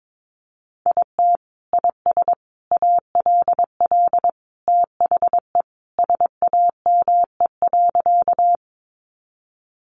Morse code, heard as a keyed tone, about 22 wpm, 700 Hz. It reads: IT IS ALL THE SAME.
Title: Sentences from Top 200 Words 22wpm